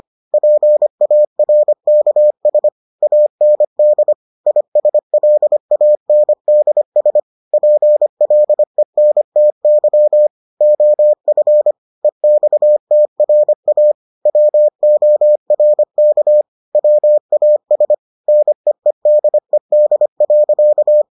Morse code, heard as a keyed tone, about 25 wpm, 600 Hz. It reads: PARKS AND ISLANDS PLENTY OF EXTRA WORK WAS NEEDED.